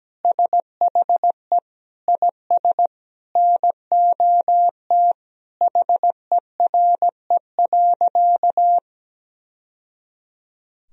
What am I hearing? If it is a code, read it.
Morse code, heard as a keyed tone, about 17 wpm, 700 Hz. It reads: SHE IS NOT HERE.
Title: Sentences from Top 1000 Words - Review Part 3 - 17wpm